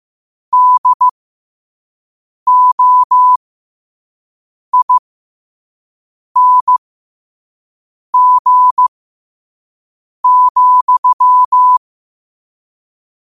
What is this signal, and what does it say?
Morse code, heard as a keyed tone, about 15 wpm, spaced as for 5 wpm, 1000 Hz. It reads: DOING,
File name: qotdfiles / 5wpm